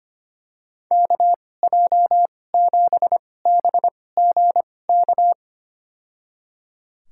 Morse code, heard as a keyed tone, about 25 wpm, spaced as for 18 wpm, 700 Hz. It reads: KJ7BGK
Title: USA Call Signs - 6 Characters - 2x Character Spacing - 25wpm